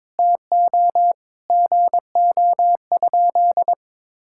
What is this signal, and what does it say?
Morse code, heard as a keyed tone, about 22 wpm, 700 Hz. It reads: TO GO?
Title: Sentences from Top 1600 Words 22wpm